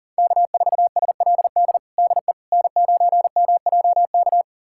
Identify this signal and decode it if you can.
Morse code, heard as a keyed tone, about 37 wpm, 700 Hz. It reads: K4SLD DE N9MJK
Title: USA Callsign DE Callsign - 5 Characters - 40wpm